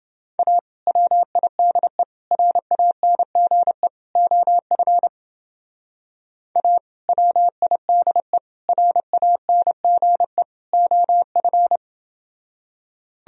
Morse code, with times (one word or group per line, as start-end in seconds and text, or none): 0.4-0.6 A
0.9-2.0 WIDE
2.3-3.9 RANGE
4.2-5.1 OF
6.6-6.8 A
7.1-8.4 WIDE
8.7-10.4 RANGE
10.7-11.8 OF